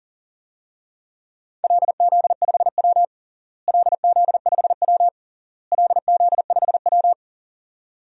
Morse code, heard as a keyed tone, about 37 wpm, 700 Hz. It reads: LZ5W LZ5W LZ5W
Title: DXCC-100 Call Signs - 4 Characters - Repeated 3x - 40wpm